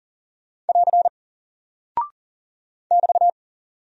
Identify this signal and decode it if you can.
Morse code, heard as a keyed tone, about 40 wpm, 700 Hz. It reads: <AR> <BT>
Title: Single Character (Letter:Number:Punctuation:Prosign) Alt 40wpm